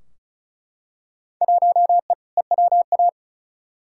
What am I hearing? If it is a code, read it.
Morse code, heard as a keyed tone, about 36 wpm, 700 Hz. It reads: 1E EWA
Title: Field Day Exchanges - 35wpm